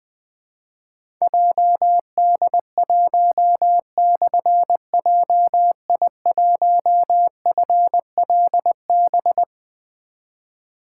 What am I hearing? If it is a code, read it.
Morse code, heard as a keyed tone, about 20 wpm, 700 Hz. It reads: JD1/JI1FLB
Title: Difficult Call Signs 20wpm